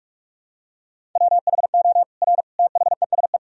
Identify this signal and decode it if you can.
Morse code, heard as a keyed tone, about 41 wpm, 700 Hz. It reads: WHO R THESE
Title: Sentences from Top 200 Words and Abbreviations 45wpm